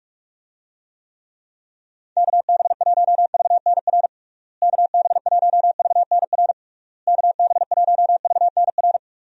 Morse code, heard as a keyed tone, about 46 wpm, 700 Hz. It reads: KB1VNR KB1VNR KB1VNR
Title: USA Call Signs - 6 Characters - Repeated 3x - 45wpm